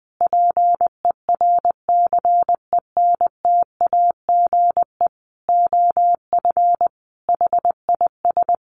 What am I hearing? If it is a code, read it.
Morse code, heard as a keyed tone, about 19 wpm, 700 Hz. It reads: PERCENTAGE OF HIS
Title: Sentences from Top 1300 Words 20wpm